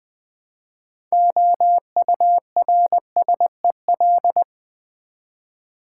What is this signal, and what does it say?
Morse code, heard as a keyed tone, about 20 wpm, 700 Hz. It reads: OURSEL
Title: Build A Word - 3 Letters At A Time - 9-Letter Words - 20wpm